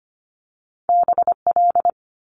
Morse code, heard as a keyed tone, about 25 wpm, 700 Hz. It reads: BL
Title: Multi-letter Phonemes Quick Fire 25wpm